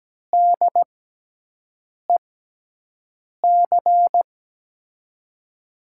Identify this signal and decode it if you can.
Morse code, heard as a keyed tone, about 17 wpm, spaced as for 6 wpm, 700 Hz. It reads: DEC
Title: USA Call Signs - 5 Characters - 6x Character Spacing - 17wpm